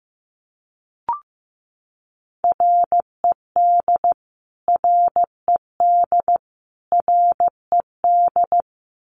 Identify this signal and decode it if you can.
Morse code, heard as a keyed tone, about 15 wpm, 700 Hz. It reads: RED RED RED
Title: Top 500 Words Repeated 15wpm